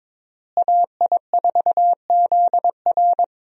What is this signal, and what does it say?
Morse code, heard as a keyed tone, about 22 wpm, 700 Hz. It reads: AI4ZR
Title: USA Call Signs - 5 Characters - Repeated 3x - 22wpm